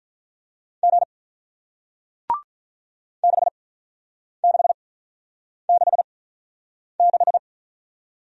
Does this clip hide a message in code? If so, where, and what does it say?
Yes: Morse code from 0.8 to 7.4 s, heard as a keyed tone, about 48 wpm, spaced as for 9 wpm, 700 Hz. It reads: G 6666